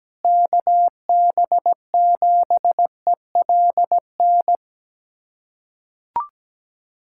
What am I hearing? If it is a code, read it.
Morse code, heard as a keyed tone, about 17 wpm, 700 Hz. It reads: KB7ELN
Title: USA Callsign DE Callsign 17wpm